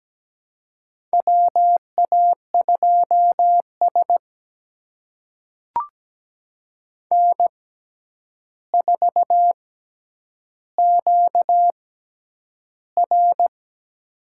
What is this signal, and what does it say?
Morse code, heard as a keyed tone, about 17 wpm, spaced as for 9 wpm, 700 Hz. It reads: WA2S N4QR
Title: USA Call Signs - 4 Characters - 6x Character Spacing - 17wpm